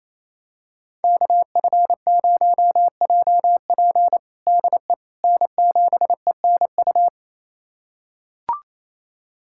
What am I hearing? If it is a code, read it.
Morse code, heard as a keyed tone, about 28 wpm, 700 Hz. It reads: KF0JP DE N7ENU
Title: USA Callsign DE Callsign - 5 Characters - 28wpm